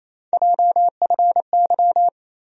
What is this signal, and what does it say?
Morse code, heard as a keyed tone, about 28 wpm, 700 Hz. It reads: JFY
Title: USA Call Sign Postfixes - 3 Letters 28wpm